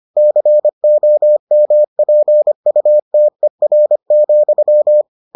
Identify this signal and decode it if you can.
Morse code, heard as a keyed tone, about 25 wpm, 600 Hz. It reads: COMPUTER,